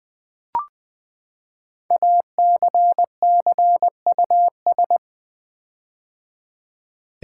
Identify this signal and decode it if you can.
Morse code, heard as a keyed tone, about 20 wpm, 700 Hz. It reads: ACCUS
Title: Build A Word - 5 Letters At A Time - 10-Letter Words - 20wpm